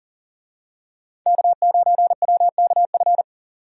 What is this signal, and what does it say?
Morse code, heard as a keyed tone, about 38 wpm, 700 Hz. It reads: K9WKF
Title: USA Call Signs - 5 Characters - 40wpm